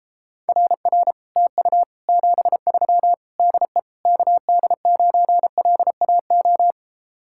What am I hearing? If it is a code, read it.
Morse code, heard as a keyed tone, about 33 wpm, 700 Hz. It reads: RR TU 73 DE KD9LAO